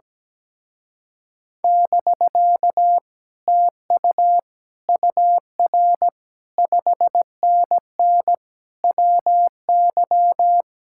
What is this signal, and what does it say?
Morse code, heard as a keyed tone, about 17 wpm, 700 Hz. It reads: <BK> TU UR 5NN WY
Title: Speed-Racing Common QSO Phrases Encore - 17wpm